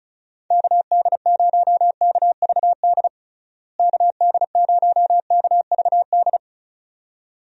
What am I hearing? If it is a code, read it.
Morse code, heard as a keyed tone, about 34 wpm, 700 Hz. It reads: KD0KVD KD0KVD
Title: USA Call Signs - 6 Characters - Repeated 2x - 35wpm